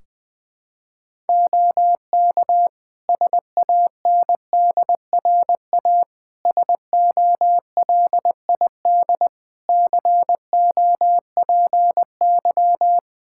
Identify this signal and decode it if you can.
Morse code, heard as a keyed tone, about 20 wpm, 700 Hz. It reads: OK SANDRA SOLID COPY
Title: Practice QSO - Exchange 3 - More Info - 20wpm